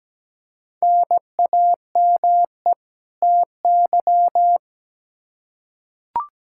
Morse code, heard as a keyed tone, about 17 wpm, 700 Hz. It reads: NAME TY
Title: Practice QSO - Exchange 1 - Beginning - 17wpm